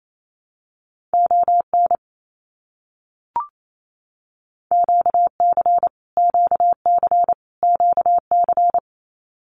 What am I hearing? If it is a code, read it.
Morse code, heard as a keyed tone, about 28 wpm, 700 Hz. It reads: ON QC QC QC
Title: Canadian Provinces and Territories - Repeated 3x - 28wpm